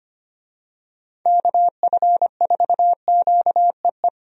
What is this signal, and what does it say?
Morse code, heard as a keyed tone, about 25 wpm, 700 Hz. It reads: KF4QEE
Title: USA Call Signs - 6 Characters - 25wpm